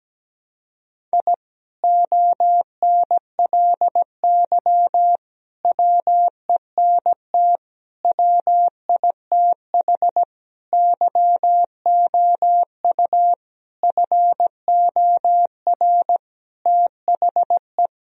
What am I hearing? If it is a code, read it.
Morse code, heard as a keyed tone, about 17 wpm, 700 Hz. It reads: I ONLY WENT WITH YOU FOR THE
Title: Sentences from Top 1000 Words - Review Part 1 - 17wpm